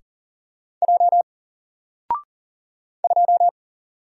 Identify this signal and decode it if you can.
Morse code, heard as a keyed tone, about 40 wpm, 700 Hz. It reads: J 2
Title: Single Letter-Number Alt 40wpm